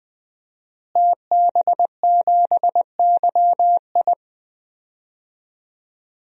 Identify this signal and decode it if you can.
Morse code, heard as a keyed tone, about 20 wpm, 700 Hz. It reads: TB7YI
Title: DXCC-100 Call Signs - 5 Characters - Repeated 2x - 20wpm